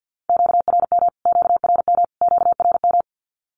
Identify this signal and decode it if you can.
Morse code, heard as a keyed tone, about 50 wpm, 700 Hz. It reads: QRM QRM QRM
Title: Q-Codes Repeated 3x 50wpm